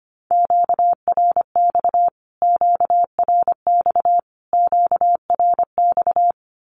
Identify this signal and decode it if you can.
Morse code, heard as a keyed tone, about 25 wpm, 700 Hz. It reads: QRX QRX QRX
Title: Q-Codes Repeated 3x 25wpm